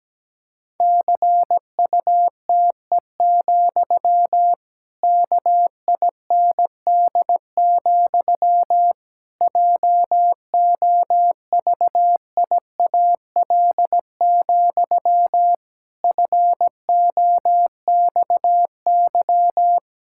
Morse code, heard as a keyed tone, about 17 wpm, 700 Hz. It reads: CUTE, KIND, JOVIAL, FOXY
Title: Pangram Challenge v2 - 17wpm